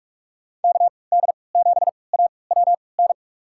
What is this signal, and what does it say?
Morse code, heard as a keyed tone, about 46 wpm, spaced as for 25 wpm, 700 Hz. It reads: KD7AWN
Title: USA Call Signs - 6 Characters - 3x Character Spacing - 45wpm